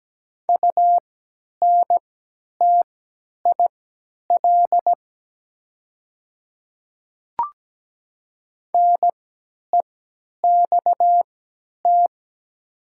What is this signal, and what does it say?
Morse code, heard as a keyed tone, about 17 wpm, spaced as for 10 wpm, 700 Hz. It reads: UNTIL NEXT
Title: 3x Character Spacing - Top 300 Words - 17wpm